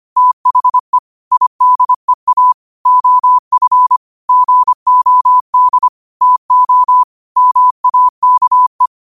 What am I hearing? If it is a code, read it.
Morse code, heard as a keyed tone, about 25 wpm, 1000 Hz. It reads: THE IDEA OF GOD TO MAKE
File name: qotdfiles / 25wpm